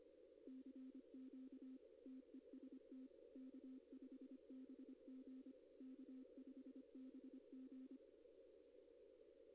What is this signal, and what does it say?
Morse code, heard as a keyed tone, about 25 wpm, 260 Hz. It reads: CQ TEST K5BG K5BG
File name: contests / arrl-160-k5tr-op-w5tn -2200